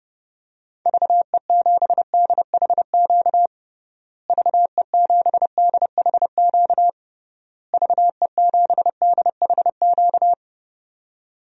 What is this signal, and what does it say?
Morse code, heard as a keyed tone, about 30 wpm, 700 Hz. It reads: VE7DHQ VE7DHQ VE7DHQ
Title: DXCC-100 Call Signs - 6 Characters - Repeated 3x - 30wpm